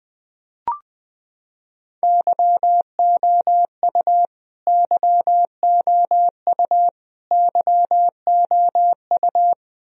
Morse code, heard as a keyed tone, about 20 wpm, 700 Hz. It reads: YOU YOU YOU
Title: Top 100 Words Repeated 20wpm